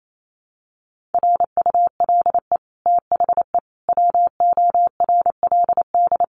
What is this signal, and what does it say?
Morse code, heard as a keyed tone, about 28 wpm, 700 Hz. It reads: RULE THE WORLD
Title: Sentences from Top 600 Words 28wpm